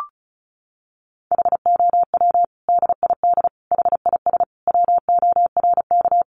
Morse code, heard as a keyed tone, about 34 wpm, 700 Hz. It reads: HOW DID HIS WORK
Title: Sentences from Top 900 Words 35wpm